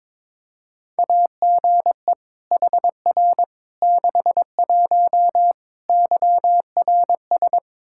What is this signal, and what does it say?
Morse code, heard as a keyed tone, about 22 wpm, 700 Hz. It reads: AGE HR 61 YRS
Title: Practice QSO - Exchange 3 - More Info - 22wpm